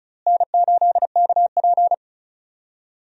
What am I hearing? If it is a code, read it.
Morse code, heard as a keyed tone, about 35 wpm, 700 Hz. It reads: N8KP
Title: USA Call Signs - 4 Characters - Repeated 2x - 35wpm